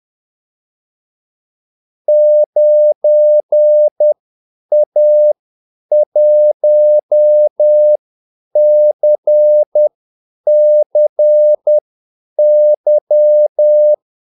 Morse code, heard as a keyed tone, about 10 wpm, spaced as for 8 wpm, 600 Hz. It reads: 9A1CCY
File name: Callsigns-10-8wpm